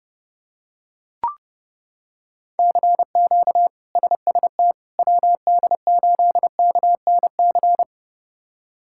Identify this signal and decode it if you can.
Morse code, heard as a keyed tone, about 32 wpm, 700 Hz. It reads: CQ SST WD8KNC
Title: K1USN SST QSO Practice - 30wpm